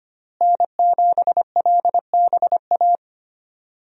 Morse code, heard as a keyed tone, about 25 wpm, 700 Hz. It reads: N7LBA
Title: USA Call Signs - 5 Characters - Repeated 2x - 25wpm